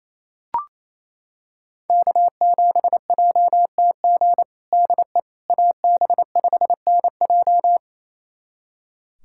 Morse code, heard as a keyed tone, about 28 wpm, 700 Hz. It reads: K7JTG DE AB5NJ
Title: USA Callsign DE Callsign - 5 Characters - 28wpm